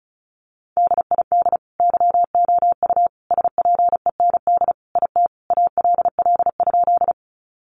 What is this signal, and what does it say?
Morse code, heard as a keyed tone, about 34 wpm, 700 Hz. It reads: DID YOU SPEND IT ALL?
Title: Sentences from Top 800 Words 35wpm